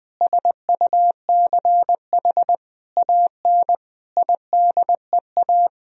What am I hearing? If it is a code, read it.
Morse code, heard as a keyed tone, about 20 wpm, 700 Hz. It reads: SUCH AN IDEA
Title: Sentences from Top 1000 Words - Review Part 1 - 20wpm